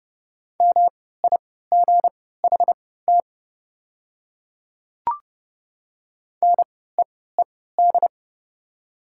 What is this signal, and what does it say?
Morse code, heard as a keyed tone, about 30 wpm, spaced as for 17 wpm, 700 Hz. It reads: MIGHT NEED
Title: 3x Character Spacing - Top 500 Words - 30wpm